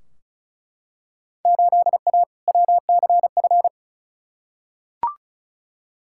Morse code, heard as a keyed tone, about 35 wpm, 690 Hz. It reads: 8A WCF
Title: Field Day Exchanges Encore - 35wpm